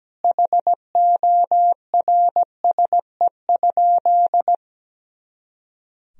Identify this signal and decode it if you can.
Morse code, heard as a keyed tone, about 17 wpm, 700 Hz. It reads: HORSE?
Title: Sentences from Top 500 Words and Abbreviations 17wpm